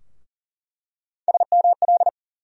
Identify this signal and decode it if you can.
Morse code, heard as a keyed tone, about 40 wpm, 700 Hz. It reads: SML
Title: USA Call Sign Postfixes - 3 Letters 40wpm